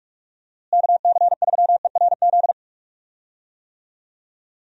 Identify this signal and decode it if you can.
Morse code, heard as a keyed tone, about 42 wpm, 700 Hz. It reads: KC3ERZ
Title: USA Call Signs - 6 Characters - Repeated 3x - 45wpm